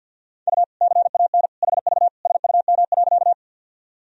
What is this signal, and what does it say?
Morse code, heard as a keyed tone, about 46 wpm, 700 Hz. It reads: U CAN HV SUM.
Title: Sentences from Top 100 Words and Abbreviations 50wpm